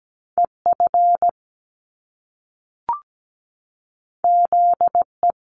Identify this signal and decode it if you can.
Morse code, heard as a keyed tone, about 17 wpm, 700 Hz. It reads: EF ZE
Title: Top Two Letter Combinations in Words 17wpm